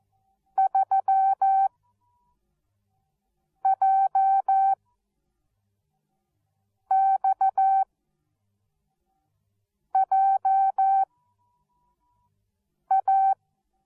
Morse code, heard as a keyed tone, about 14 wpm, spaced as for 4 wpm, 770 Hz. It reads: 3JXJA
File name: Onlinelehrgang / morsen / DJ4UF Morsen13